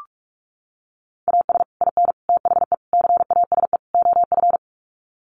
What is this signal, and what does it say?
Morse code, heard as a keyed tone, about 44 wpm, 700 Hz. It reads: AS IN THE CASE OF
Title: Sets of 5 Words 45wpm